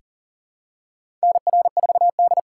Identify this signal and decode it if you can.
Morse code, heard as a keyed tone, about 40 wpm, 700 Hz. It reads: NR4D
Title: USA Call Signs - 4 Characters - 4x Character Spacing - 40wpm